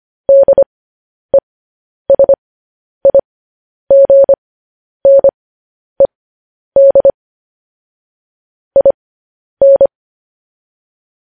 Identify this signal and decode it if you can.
Morse code, heard as a keyed tone, about 25 wpm, spaced as for 10 wpm, 550 Hz. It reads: DESIGNED IN
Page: news headlines in morse code at 10 wpm